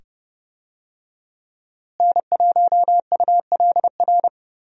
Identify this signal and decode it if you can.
Morse code, heard as a keyed tone, about 30 wpm, 700 Hz. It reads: N1ULR
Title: USA Call Signs - 5 Characters - Repeated 2x - 30wpm